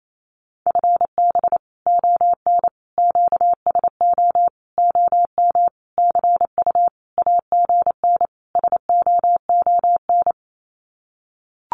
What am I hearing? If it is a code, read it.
Morse code, heard as a keyed tone, about 28 wpm, 700 Hz. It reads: FB ON QSO OM CU AGN SOON
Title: Common QSO Phrases 28wpm